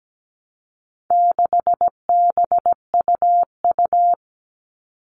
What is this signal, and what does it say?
Morse code, heard as a keyed tone, about 17 wpm, 700 Hz. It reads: 6BUU
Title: All Cut Numbers - 4-Digit 17wpm